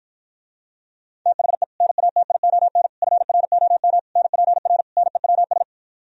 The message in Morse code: THE NATION FROM NPR NEWS

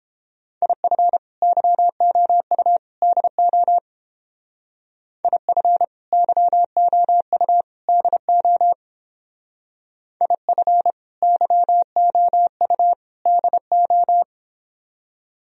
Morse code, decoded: IF YOU DO IF YOU DO IF YOU DO

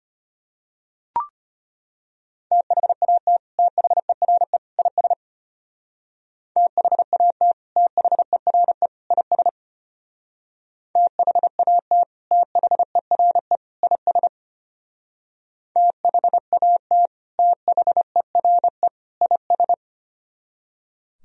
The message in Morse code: THAT THERE IS THAT THERE IS THAT THERE IS THAT THERE IS

